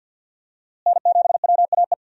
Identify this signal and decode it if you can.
Morse code, heard as a keyed tone, about 46 wpm, 700 Hz. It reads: N7WAE
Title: USA Call Signs - 5 Characters - Repeated 2x - 50wpm